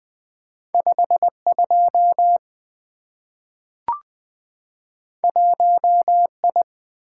Morse code, heard as a keyed tone, about 20 wpm, 700 Hz. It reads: 52 1I
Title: Two Letter-Number Combinations - 20wpm